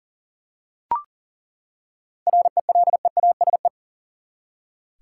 Morse code, heard as a keyed tone, about 40 wpm, 700 Hz. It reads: RELEASE
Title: Seven Letter Words - 40wpm